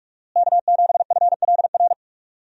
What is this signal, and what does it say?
Morse code, heard as a keyed tone, about 46 wpm, 700 Hz. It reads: K7FLR
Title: USA Call Signs - 5 Characters - Repeated 3x - 45wpm